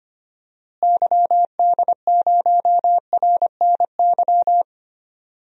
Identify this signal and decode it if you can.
Morse code, heard as a keyed tone, about 25 wpm, 700 Hz. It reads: YD0RNY
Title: DXCC-100 Call Signs - 6 Characters - Repeated 3x - 25wpm